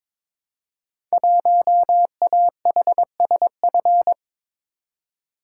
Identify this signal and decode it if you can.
Morse code, heard as a keyed tone, about 22 wpm, 700 Hz. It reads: 1AHSF
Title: Code Groups - 5 Letter:Numbers - 22wpm